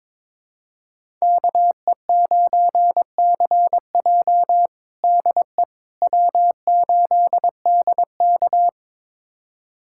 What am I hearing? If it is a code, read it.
Morse code, heard as a keyed tone, about 22 wpm, 700 Hz. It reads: KE9CJ DE W8DK
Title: USA Callsign DE Callsign 22wpm